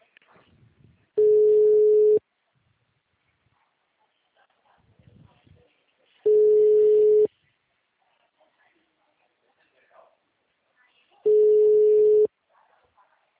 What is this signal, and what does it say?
Ringback tone (Europe, 425 Hz), with speech.